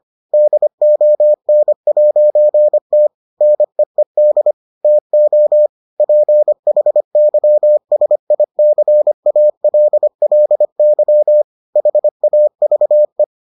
Morse code, heard as a keyed tone, about 25 wpm, 600 Hz. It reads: DON'T NEED TO PHYSICALLY HAVE